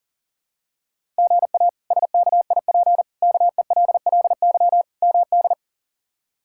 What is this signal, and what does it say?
Morse code, heard as a keyed tone, about 40 wpm, 700 Hz. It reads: GA SKIP KELLY MD
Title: K1USN SST QSO Practice - 40wpm